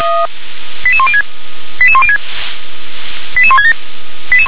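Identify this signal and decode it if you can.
Five-tone selective call (ZVEI), heard as quick runs of five tones, tones of about 70 ms. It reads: 88106 88106 88167
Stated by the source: ZVEI five-tone selective call received on SDR.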